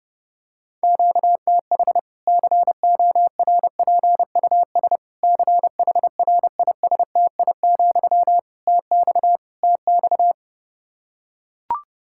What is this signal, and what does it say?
Morse code, heard as a keyed tone, about 30 wpm, 700 Hz. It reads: QTH CORPUS CHRISTI, TX TX
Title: Practice QSO - Exchange 1 - Beginning - 30wpm